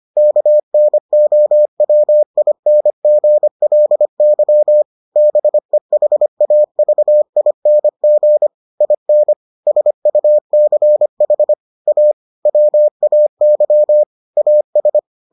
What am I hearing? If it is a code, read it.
Morse code, heard as a keyed tone, about 25 wpm, 600 Hz. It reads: KNOWINGLY BEHAVING IN SUCH A WAY AS